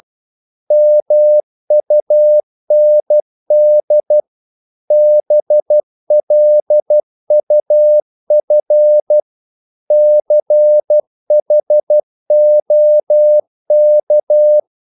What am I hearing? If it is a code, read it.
Morse code, heard as a keyed tone, about 12 wpm, 600 Hz. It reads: MUND BLUF CHOK
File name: Korte ord DK 12wpm